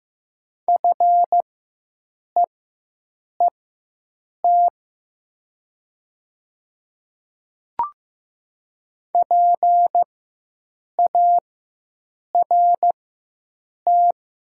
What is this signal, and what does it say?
Morse code, heard as a keyed tone, about 15 wpm, spaced as for 7 wpm, 700 Hz. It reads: FEET PART
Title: 4x Character Spacing - Top 500 Words - 15wpm